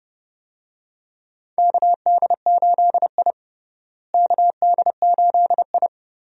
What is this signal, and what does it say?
Morse code, heard as a keyed tone, about 30 wpm, 700 Hz. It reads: KD8I KD8I
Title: USA Call Signs - 4 Characters - Repeated 2x - 30wpm